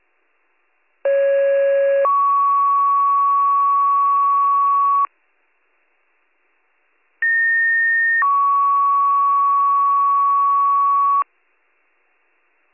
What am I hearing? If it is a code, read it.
Two-tone page, heard as a one-second tone, then a three-second tone, about 550 then 1120 Hz, about 1850 then 1120 Hz. It is heard from 1.0 to 11.2 s.